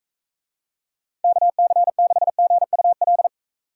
Morse code, heard as a keyed tone, about 42 wpm, 700 Hz. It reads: KC6GUL